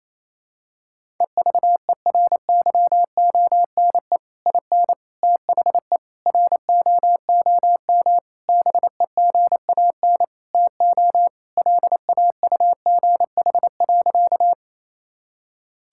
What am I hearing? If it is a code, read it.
Morse code, heard as a keyed tone, about 28 wpm, 700 Hz. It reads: EVERYONE IN THE ROOM BEGAN TO LAUGH.
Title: Sentences from Top 1000 Words - Review Part 5 - 28wpm